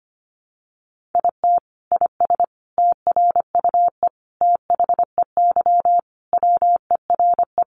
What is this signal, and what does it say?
Morse code, heard as a keyed tone, about 25 wpm, 700 Hz. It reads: IT IS TRUE THEY WERE
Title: Sentences from Top 1400 Words 25wpm